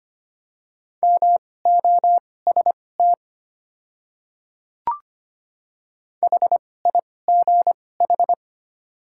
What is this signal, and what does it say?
Morse code, heard as a keyed tone, about 25 wpm, spaced as for 18 wpm, 690 Hz. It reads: MOST HIGH
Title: 2x Character Spacing - Top 200 Words - 25wpm